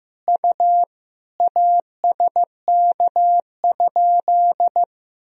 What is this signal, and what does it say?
Morse code, heard as a keyed tone, about 16 wpm, 700 Hz. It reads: U ASK?
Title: Sentences from Top 200 Words and Abbreviations 15wpm